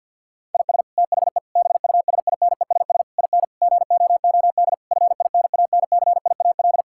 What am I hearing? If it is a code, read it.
Morse code, heard as a keyed tone, about 49 wpm, 700 Hz. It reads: IS THE BUSINESS IN GOOD FINANCIAL
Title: Sentences from Top 1000 Words - Review Part 5 - 50wpm